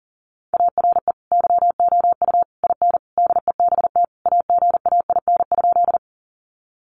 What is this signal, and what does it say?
Morse code, heard as a keyed tone, about 40 wpm, 700 Hz. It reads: ARE YOU IN DEBT AGAIN?
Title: Sentences from Top 1300 Words 40wpm